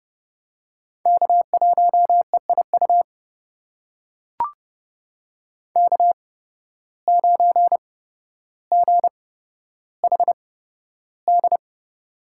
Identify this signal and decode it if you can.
Morse code, heard as a keyed tone, about 32 wpm, spaced as for 13 wpm, 700 Hz. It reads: K1EIU K9GHD
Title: USA Call Signs - 5 Characters - 8x Character Spacing - 30wpm